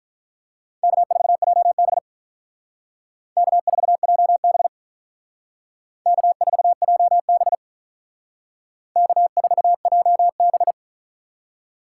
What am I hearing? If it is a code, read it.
Morse code, heard as a keyed tone, about 46 wpm, 700 Hz. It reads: K4JB K4JB K4JB K4JB